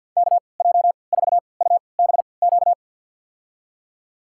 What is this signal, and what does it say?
Morse code, heard as a keyed tone, about 50 wpm, spaced as for 28 wpm, 700 Hz. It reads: KJ4UBQ
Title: USA Call Signs - 6 Characters - 3x Character Spacing - 50wpm